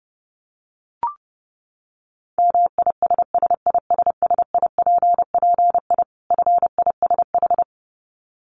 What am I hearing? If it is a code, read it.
Morse code, heard as a keyed tone, about 30 wpm, 700 Hz. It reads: MISSISSIPPI FISH
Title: Dit Challenge 30wpm